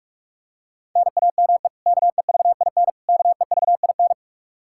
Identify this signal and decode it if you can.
Morse code, heard as a keyed tone, about 46 wpm, 700 Hz. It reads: NAME KEVIN KEVIN
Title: Practice QSO - Exchange 1 - Beginning - 45wpm